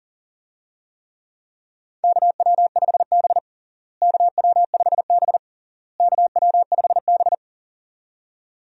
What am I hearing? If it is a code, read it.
Morse code, heard as a keyed tone, about 40 wpm, 700 Hz. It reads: KW5B KW5B KW5B
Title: USA Call Signs - 4 Characters - Repeated 3x - 40wpm